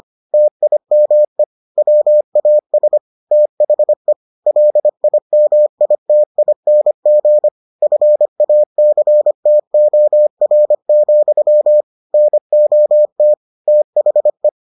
Morse code, heard as a keyed tone, about 25 wpm, 600 Hz. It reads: TIME WAS THE LIMITING FACTOR, NOT THE